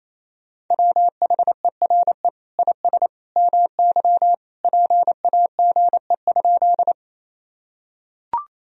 Morse code, heard as a keyed tone, about 28 wpm, 700 Hz. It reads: WHERE IS MY PAGE?
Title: Sentences from Top 1000 Words - Review Part 1 - 28wpm